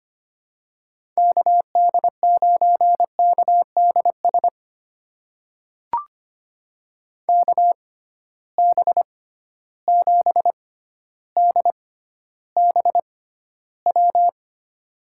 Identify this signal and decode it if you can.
Morse code, heard as a keyed tone, about 25 wpm, spaced as for 13 wpm, 700 Hz. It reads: KD9KDS KB7DBW
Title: USA Call Signs - 6 Characters - 6x Character Spacing - 25wpm